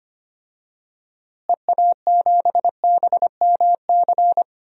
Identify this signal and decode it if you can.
Morse code, heard as a keyed tone, about 25 wpm, 700 Hz. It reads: EA7BMC
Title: DXCC-100 Call Signs - 6 Characters - Repeated 2x - 25wpm